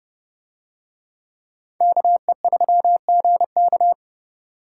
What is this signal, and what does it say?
Morse code, heard as a keyed tone, about 29 wpm, 700 Hz. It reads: KE3GK